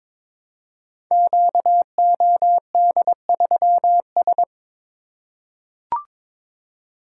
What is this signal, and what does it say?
Morse code, heard as a keyed tone, about 22 wpm, 700 Hz. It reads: QOD3S
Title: Code Groups - 5 Letter:Numbers - 22wpm